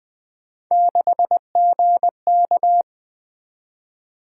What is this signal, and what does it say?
Morse code, heard as a keyed tone, about 21 wpm, 700 Hz. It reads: 6GK
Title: Three Letter-Number Combinations - 20wpm